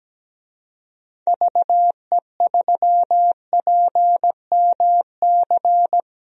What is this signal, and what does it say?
Morse code, heard as a keyed tone, about 17 wpm, 700 Hz. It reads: VE3PMC
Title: DXCC-100 Call Signs - 6 Characters - Repeated 3x - 17wpm